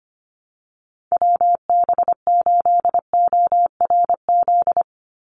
Morse code, heard as a keyed tone, about 25 wpm, 700 Hz. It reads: WB8ORZ
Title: USA Call Signs 25wpm